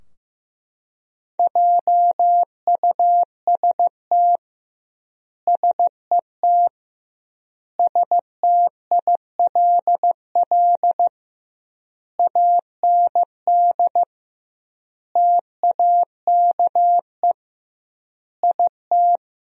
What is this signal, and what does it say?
Morse code, heard as a keyed tone, about 15 wpm, 700 Hz. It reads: JUST SET STILL AND TAKE IT
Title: 2x Word-Spacing - Sentences from Top 300 Words - 15wpm